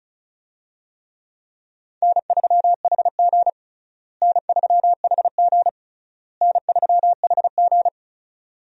Morse code, heard as a keyed tone, about 36 wpm, 700 Hz. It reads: N3HG N3HG N3HG
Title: USA Call Signs - 4 Characters - Repeated 3x - 35wpm